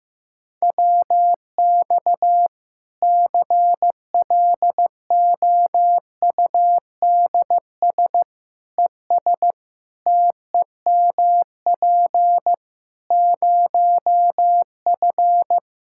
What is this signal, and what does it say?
Morse code, heard as a keyed tone, about 15 wpm, 700 Hz. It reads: WX CLOUDS ES TEMP 0F
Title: Practice QSO - Exchange 2 - Info - 15wpm